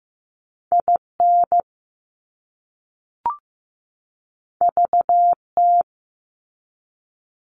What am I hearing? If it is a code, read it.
Morse code, heard as a keyed tone, about 15 wpm, 700 Hz. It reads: IN VT
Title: US States 15wpm